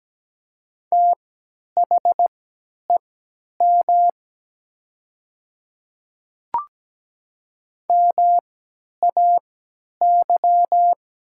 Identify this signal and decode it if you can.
Morse code, heard as a keyed tone, about 17 wpm, spaced as for 10 wpm, 700 Hz. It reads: THEM MAY
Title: 3x Character Spacing - Top 100 Words - 17wpm